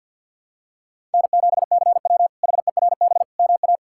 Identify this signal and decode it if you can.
Morse code, heard as a keyed tone, about 51 wpm, 700 Hz. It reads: N7CW HERB MA
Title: K1USN SST QSO Practice - 50wpm